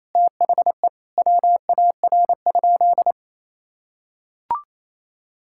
Morse code, heard as a keyed tone, about 28 wpm, 700 Hz. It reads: THE WAR?
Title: Sentences from Top 1000 Words - Review Part 6 - 28wpm